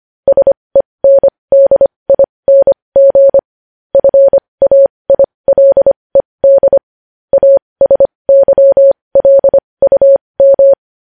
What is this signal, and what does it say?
Morse code, heard as a keyed tone, about 25 wpm, spaced as for 20 wpm, 550 Hz. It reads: SENDING FAILED ASYLUM
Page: news-headlines-in-morse-code-at-20-wpm